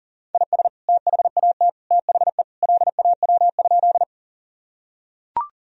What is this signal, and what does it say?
Morse code, heard as a keyed tone, about 40 wpm, 700 Hz. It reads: IS THAT THE LAW?
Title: Sentences from Top 1000 Words - Review Part 3 - 40wpm